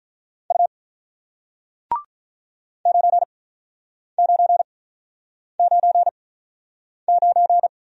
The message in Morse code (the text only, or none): U 9999